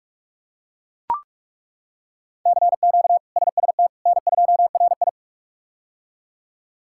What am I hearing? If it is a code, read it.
Morse code, heard as a keyed tone, about 42 wpm, 700 Hz. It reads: CQ SST N2RI
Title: K1USN SST QSO Practice - 45wpm